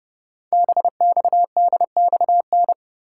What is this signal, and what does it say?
Morse code, heard as a keyed tone, about 30 wpm, 700 Hz. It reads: BXDXN
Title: Code Groups - 5 Letter:Numbers - 30wpm